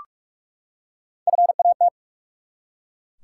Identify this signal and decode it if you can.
Morse code, heard as a keyed tone, about 44 wpm, 700 Hz. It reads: FAT
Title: Three Letter Words - 45wpm